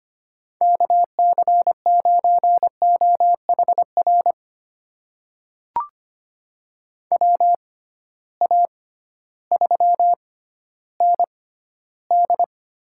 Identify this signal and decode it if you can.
Morse code, heard as a keyed tone, about 25 wpm, spaced as for 13 wpm, 700 Hz. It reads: KC9OHR WA3ND
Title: USA Call Signs - 6 Characters - 6x Character Spacing - 25wpm